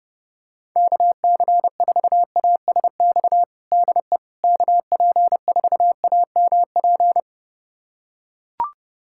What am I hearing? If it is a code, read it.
Morse code, heard as a keyed tone, about 30 wpm, 700 Hz. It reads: KC4ASX DE KP4AMP
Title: USA Callsign DE Callsign 30wpm